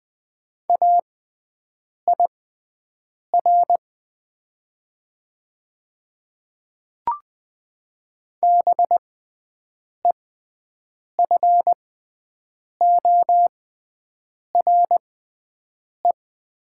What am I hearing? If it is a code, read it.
Morse code, heard as a keyed tone, about 21 wpm, spaced as for 7 wpm, 700 Hz. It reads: AIR BEFORE
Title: 6x Character Spacing - Top 200 Words - 20wpm